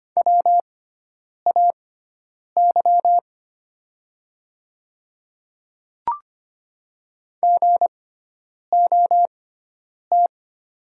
Morse code, heard as a keyed tone, about 25 wpm, spaced as for 9 wpm, 700 Hz. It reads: WAY GOT
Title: 6x Character Spacing - Top 300 Words - 25wpm